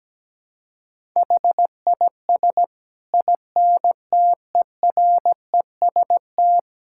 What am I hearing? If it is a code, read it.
Morse code, heard as a keyed tone, about 17 wpm, 700 Hz. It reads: HIS INTEREST
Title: Sentences from Top 1000 Words - Review Part 5 - 17wpm